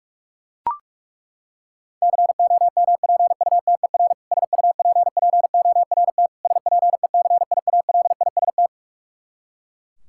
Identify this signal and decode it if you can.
Morse code, heard as a keyed tone, about 41 wpm, 700 Hz. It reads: COMPUTER SUPPORT SPECIALIST
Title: Practice QSO - Exchange 3 - More Info - 45wpm